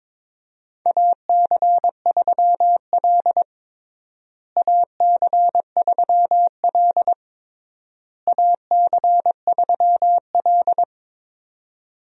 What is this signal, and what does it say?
Morse code, heard as a keyed tone, about 22 wpm, 700 Hz. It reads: AC3L AC3L AC3L
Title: USA Call Signs - 4 Characters - Repeated 3x - 22wpm